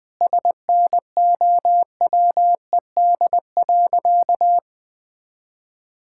Morse code, heard as a keyed tone, about 21 wpm, 700 Hz. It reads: SNOWED.
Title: Sentences with 5-Word Phrases - 20wpm